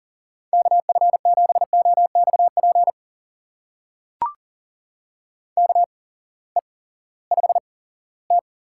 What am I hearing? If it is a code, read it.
Morse code, heard as a keyed tone, about 40 wpm, spaced as for 20 wpm, 700 Hz. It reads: KF7OXP KE5T
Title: USA Call Signs - 6 Characters - 8x Character Spacing - 40wpm